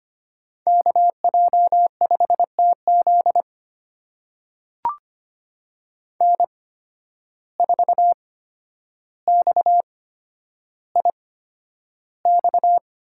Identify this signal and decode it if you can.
Morse code, heard as a keyed tone, about 25 wpm, spaced as for 11 wpm, 700 Hz. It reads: KJ5TZ N4XIX
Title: USA Call Signs - 5 Characters - 8x Character Spacing - 25wpm